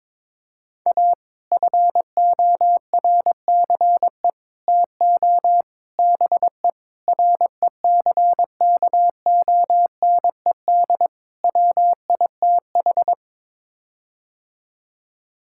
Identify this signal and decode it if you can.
Morse code, heard as a keyed tone, about 22 wpm, 700 Hz. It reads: A FORCE TO BE RECKONED WITH
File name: practice / 22wpm / Sets of 6 Words - 22wpm